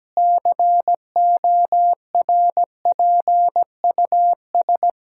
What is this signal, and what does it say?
Morse code, heard as a keyed tone, about 17 wpm, 700 Hz. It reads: CORPUS
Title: Practice QSO - Exchange 1 - Beginning - 17wpm